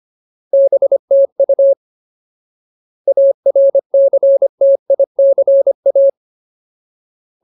Morse code, heard as a keyed tone, about 25 wpm, 550 Hz. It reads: BTU ARCTICA
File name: Ultimate Word List / 25WPM Ultimate